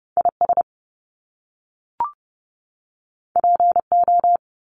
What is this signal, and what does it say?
Morse code, heard as a keyed tone, about 30 wpm, 700 Hz. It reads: IS PO
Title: Top Two Letter Combinations in Words 30wpm